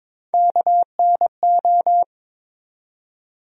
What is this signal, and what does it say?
Morse code, heard as a keyed tone, about 22 wpm, 700 Hz. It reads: KNO
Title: USA Call Sign Postfixes - 3 Letters 22wpm